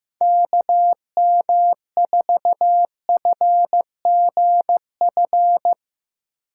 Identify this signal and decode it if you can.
Morse code, heard as a keyed tone, about 15 wpm, 700 Hz. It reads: KM4FGF
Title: USA Call Signs - 6 Characters - 8x Character Spacing - 15wpm